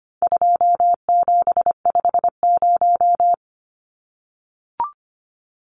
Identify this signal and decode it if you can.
Morse code, heard as a keyed tone, about 25 wpm, 700 Hz. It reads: 2750
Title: Numbers - 4-Digit 25wpm